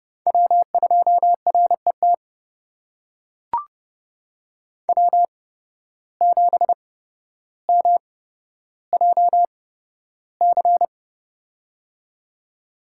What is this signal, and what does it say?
Morse code, heard as a keyed tone, about 32 wpm, spaced as for 13 wpm, 700 Hz. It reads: W2RET W7MJC
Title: USA Call Signs - 5 Characters - 8x Character Spacing - 30wpm